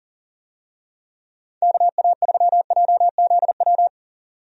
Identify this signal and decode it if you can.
Morse code, heard as a keyed tone, about 40 wpm, 700 Hz. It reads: KA3JZW
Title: USA Call Signs - 6 Characters - 40wpm